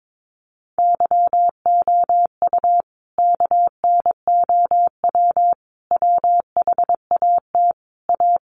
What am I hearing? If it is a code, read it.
Morse code, heard as a keyed tone, about 22 wpm, 700 Hz. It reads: YOU KNOW WHAT A
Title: Sentences from Top 1200 Words 22wpm